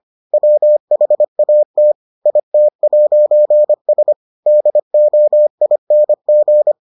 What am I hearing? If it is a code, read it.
Morse code, heard as a keyed tone, about 25 wpm, 600 Hz. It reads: WHAT IT'S DOING